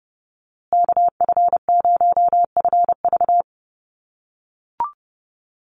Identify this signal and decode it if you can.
Morse code, heard as a keyed tone, about 30 wpm, 700 Hz. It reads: KF0FV
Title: USA Call Signs Repeated 3x 30wpm